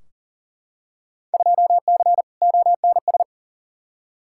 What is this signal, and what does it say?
Morse code, heard as a keyed tone, about 40 wpm, 700 Hz. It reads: 2C ONS
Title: Field Day Exchanges Encore - 40wpm